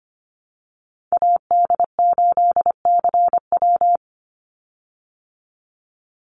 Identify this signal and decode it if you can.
Morse code, heard as a keyed tone, about 25 wpm, 700 Hz. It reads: AD8CW
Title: Call Signs Repeated 3x - 25wpm